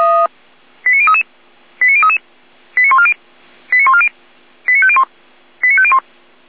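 Five-tone selective call (ZVEI), heard as quick runs of five tones, tones of about 70 ms. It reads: 89033 89033 89140 89140 89501 89501
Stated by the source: SDR receiver recording of ZVEI five-tone selective call 89120.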